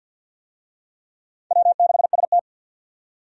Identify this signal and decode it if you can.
Morse code, heard as a keyed tone, about 48 wpm, 700 Hz. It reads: W6ST
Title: USA Call Signs - 4 Characters - 50wpm